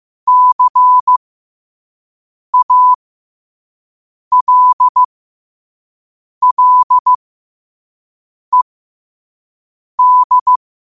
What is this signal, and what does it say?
Morse code, heard as a keyed tone, about 15 wpm, spaced as for 5 wpm, 1000 Hz. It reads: CALLED
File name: qotdfiles / 5wpm